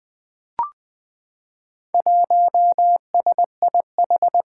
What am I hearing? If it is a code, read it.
Morse code, heard as a keyed tone, about 20 wpm, 700 Hz. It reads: 1SIH
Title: Four Challenging Letter-Number Combinations 20wpm